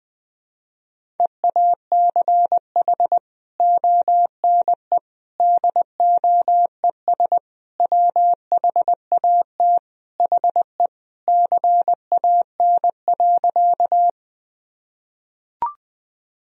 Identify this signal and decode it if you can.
Morse code, heard as a keyed tone, about 20 wpm, 700 Hz. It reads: EACH ONE DOES WHAT HE CAN.
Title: Sentences from Top 1000 Words - Review Part 1 - 20wpm